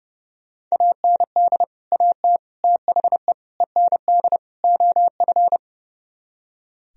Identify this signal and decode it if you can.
Morse code, heard as a keyed tone, about 36 wpm, 700 Hz. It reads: AND AT THE END OF
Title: Speed-Racing - Sets of 5 Words 30wpm